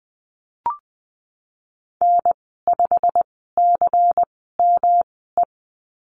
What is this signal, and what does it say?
Morse code, heard as a keyed tone, about 20 wpm, spaced as for 15 wpm, 700 Hz. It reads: N5CME
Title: 140 Lesson 23g - Letter f - Call Signs 20wpm